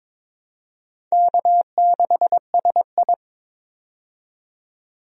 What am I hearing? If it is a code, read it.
Morse code, heard as a keyed tone, about 22 wpm, 700 Hz. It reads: K6SI
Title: USA Call Signs - 4 Characters - 22wpm